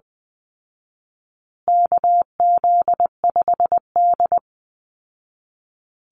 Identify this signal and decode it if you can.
Morse code, heard as a keyed tone, about 20 wpm, 700 Hz. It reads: KZ5D
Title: Call Signs Repeated 2x - 20wpm